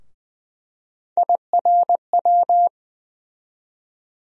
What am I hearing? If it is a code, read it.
Morse code, heard as a keyed tone, about 20 wpm, 700 Hz. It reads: IRW
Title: USA Call Sign Postfixes - 3 Letters 20wpm